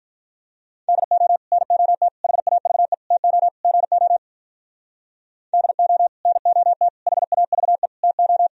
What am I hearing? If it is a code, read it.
Morse code, heard as a keyed tone, about 51 wpm, 700 Hz. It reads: DO NOT HAVE TO GO DO NOT HAVE TO